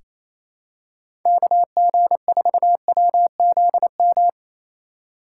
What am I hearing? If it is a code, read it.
Morse code, heard as a keyed tone, about 28 wpm, 700 Hz. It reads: KG4WZM
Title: USA Call Signs - 6 Characters - 3x Character Spacing - 28wpm